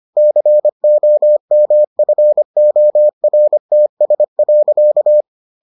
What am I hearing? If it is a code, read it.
Morse code, heard as a keyed tone, about 25 wpm, 600 Hz. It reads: COMFORTS.